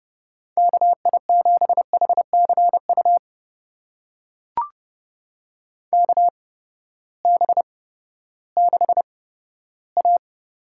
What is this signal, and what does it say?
Morse code, heard as a keyed tone, about 30 wpm, spaced as for 15 wpm, 700 Hz. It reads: KI7HCU KB6A
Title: USA Call Signs - 6 Characters - 8x Character Spacing - 30wpm